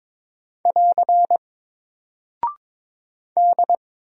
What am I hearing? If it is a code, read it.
Morse code, heard as a keyed tone, about 22 wpm, 700 Hz. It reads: <AR> D
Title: Single Character (Letter:Number:Punctuation:Prosign) Alt 22wpm